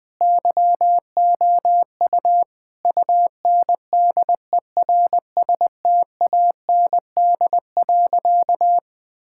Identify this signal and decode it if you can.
Morse code, heard as a keyed tone, about 19 wpm, 700 Hz. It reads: YOU UNDERSTAND.
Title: Sentences with 4-Word Phrases - 20wpm